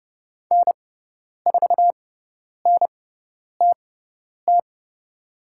Morse code, heard as a keyed tone, about 30 wpm, spaced as for 10 wpm, 700 Hz. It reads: N4NTT
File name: Lesson 17g - Letter u - Call Signs 30wpm